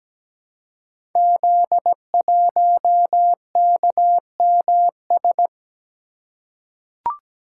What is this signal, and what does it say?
Morse code, heard as a keyed tone, about 17 wpm, 700 Hz. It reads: Z1KMS
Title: Code Groups - 5 Letter:Numbers - 17wpm